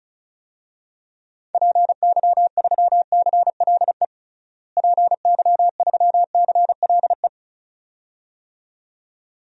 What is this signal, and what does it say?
Morse code, heard as a keyed tone, about 36 wpm, 700 Hz. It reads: PY3CLE PY3CLE
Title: DXCC-100 Call Signs - 6 Characters - Repeated 2x - 35wpm